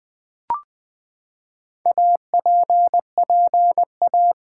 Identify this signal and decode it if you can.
Morse code, heard as a keyed tone, about 20 wpm, 700 Hz. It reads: APPA
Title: Build A Word - 4 Letters At A Time - 8-Letter Words - 20wpm